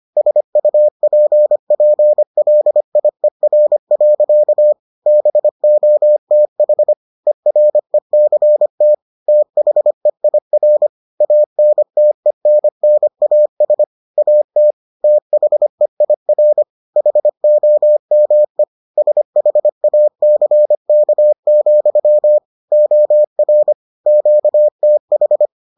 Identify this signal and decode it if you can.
Morse code, heard as a keyed tone, about 25 wpm, 600 Hz. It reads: SUPPLIER. BOTH ERECT THEIR ANTENNAS AT THEIR HOME SHACK, OR QTH